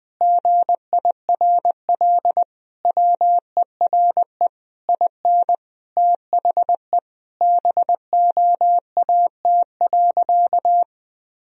Morse code, heard as a keyed tone, about 21 wpm, 700 Hz. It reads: GIRL WERE IN THE BOAT.
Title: Sentences from Top 1000 Words - Review Part 3 - 20wpm